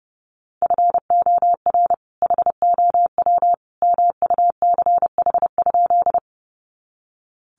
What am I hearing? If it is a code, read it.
Morse code, heard as a keyed tone, about 30 wpm, 700 Hz. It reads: FOR HOW MUCH?
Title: Sentences from Top 200 Words 30wpm